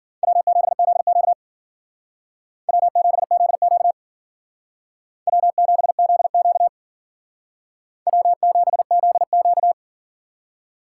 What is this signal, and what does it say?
Morse code, heard as a keyed tone, about 52 wpm, 700 Hz. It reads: W7ZQ W7ZQ W7ZQ W7ZQ